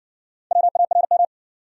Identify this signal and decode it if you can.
Morse code, heard as a keyed tone, about 60 wpm, 700 Hz. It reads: WARM